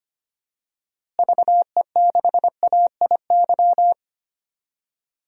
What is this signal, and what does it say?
Morse code, heard as a keyed tone, about 25 wpm, 700 Hz. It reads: VE6AIY
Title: DXCC-100 Call Signs - 6 Characters - Repeated 3x - 25wpm